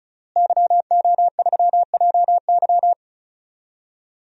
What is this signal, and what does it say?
Morse code, heard as a keyed tone, about 36 wpm, 700 Hz. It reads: YO3JY
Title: DXCC-100 Call Signs - 5 Characters - Repeated 3x - 35wpm